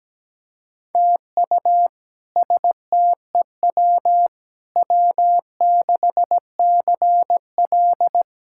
Morse code, heard as a keyed tone, about 17 wpm, 700 Hz. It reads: TU STEW W6CL
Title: K1USN SST QSO Practice - 17wpm